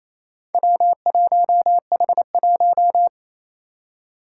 Morse code, heard as a keyed tone, about 28 wpm, 700 Hz. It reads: W1H1
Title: Four Challenging Letter-Number Combinations 28wpm